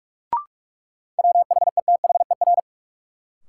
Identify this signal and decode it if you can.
Morse code, heard as a keyed tone, about 45 wpm, 700 Hz. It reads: WHETHER
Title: Sets of 3 Words - New Word Drill - 45wpm